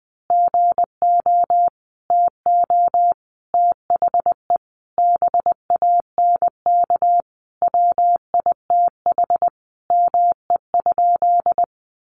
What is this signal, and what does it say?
Morse code, heard as a keyed tone, about 20 wpm, 700 Hz. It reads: GO TO THE BANK WITH ME?
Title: Sentences from Top 900 Words 20wpm